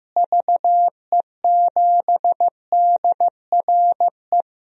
Morse code, heard as a keyed tone, about 15 wpm, 700 Hz. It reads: VE7DRE
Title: DXCC-100 Call Signs - 6 Characters - Repeated 2x - 15wpm